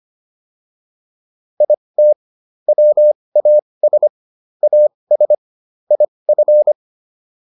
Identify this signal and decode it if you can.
Morse code, heard as a keyed tone, about 25 wpm, spaced as for 20 wpm, 600 Hz. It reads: IT WAS AS IF